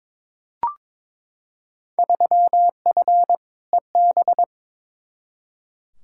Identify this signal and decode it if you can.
Morse code, heard as a keyed tone, about 22 wpm, 700 Hz. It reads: 3F EB
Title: Field Day Exchanges - 22wpm